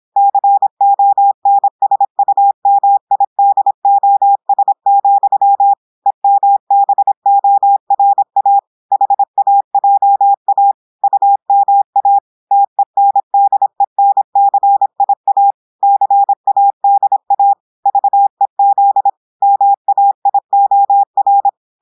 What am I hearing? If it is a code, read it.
Morse code, heard as a keyed tone, about 26 wpm, 800 Hz. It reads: CONSUMIDOS, EMBORA HAJA UMA TENDENCIA CADA VEZ MAIOR
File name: portugais texte en clair 2 vitesse 26 mots minutes